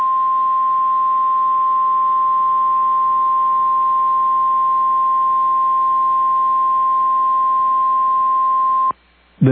Emergency Alert System (SAME) tones (the 1050 Hz attention tone). It is heard from 1.1 to 8.5 s.
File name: EAS Logs / KMNR